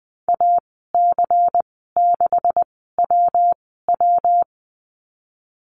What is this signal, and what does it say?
Morse code, heard as a keyed tone, about 20 wpm, spaced as for 14 wpm, 700 Hz. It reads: AC6WW
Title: 128 Lesson 21g - Number 6 - Call Signs 20wpm